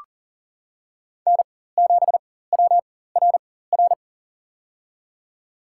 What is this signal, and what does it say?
Morse code, heard as a keyed tone, about 40 wpm, spaced as for 19 wpm, 700 Hz. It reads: N7WRR
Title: USA Call Signs - 4x Character Spacing - 40wpm